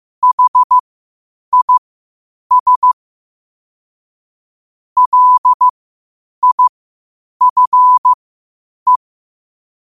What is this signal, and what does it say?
Morse code, heard as a keyed tone, about 15 wpm, spaced as for 8 wpm, 1000 Hz. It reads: HIS LIFE
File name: qotdfiles / 8wpm